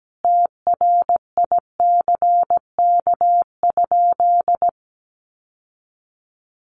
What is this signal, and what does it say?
Morse code, heard as a keyed tone, about 17 wpm, 700 Hz. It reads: TRICK?
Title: Sentences from Top 1500 Words 17wpm